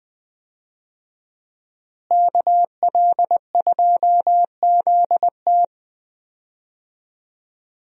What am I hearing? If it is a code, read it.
Morse code, heard as a keyed tone, about 20 wpm, 700 Hz. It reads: KL2ZT
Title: USA Call Signs - 5 Characters - Repeated 2x - 20wpm